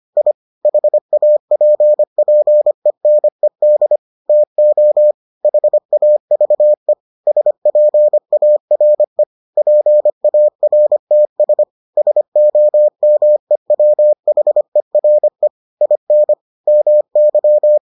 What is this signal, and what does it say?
Morse code, heard as a keyed tone, about 25 wpm, 600 Hz. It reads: I HAPPENED TO HAVE SPARE PARTS SOMEWHERE IN MY